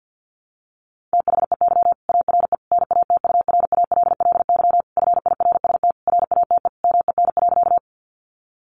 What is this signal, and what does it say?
Morse code, heard as a keyed tone, about 50 wpm, 700 Hz. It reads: THEY ARE NATURALLY FIRST RATE MEN.
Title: Sentences from Top 1300 Words 50wpm